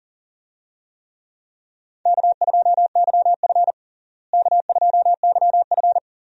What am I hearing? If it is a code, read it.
Morse code, heard as a keyed tone, about 37 wpm, 700 Hz. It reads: K2YF K2YF
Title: USA Call Signs - 4 Characters - Repeated 2x - 40wpm